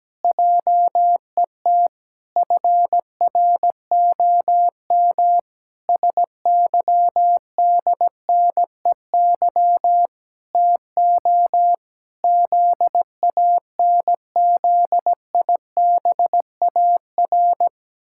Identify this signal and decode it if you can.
Morse code, heard as a keyed tone, about 17 wpm, 700 Hz. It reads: JET FROM SYDNEY TO ZANZIBAR
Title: Pangram Challenge v2 - 17wpm